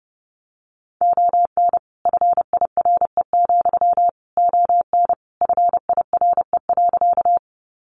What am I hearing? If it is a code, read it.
Morse code, heard as a keyed tone, about 30 wpm, 700 Hz. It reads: ON FIRE, ON FIRE.
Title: Sentences from Top 400 Words 30wpm